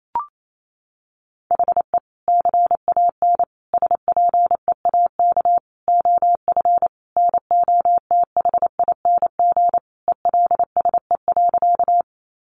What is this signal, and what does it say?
Morse code, heard as a keyed tone, about 28 wpm, 700 Hz. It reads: HE CAN SPEAK OF NOTHING ELSE.
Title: Sentences from Top 1100 Words 28wpm